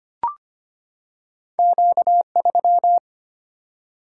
Morse code, heard as a keyed tone, about 25 wpm, 700 Hz. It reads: Q3
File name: Two Letter-Number Combinations - 25wpm